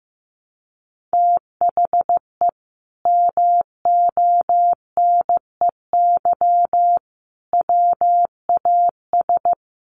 Morse code, heard as a keyed tone, about 15 wpm, 700 Hz. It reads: THE MONEY WAS
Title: Sentences from Top 1400 Words 15wpm